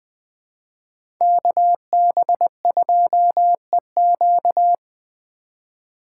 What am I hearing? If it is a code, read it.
Morse code, heard as a keyed tone, about 20 wpm, 700 Hz. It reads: KB2EQ
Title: Speed Racing - USA 2x2 Call Signs 17wpm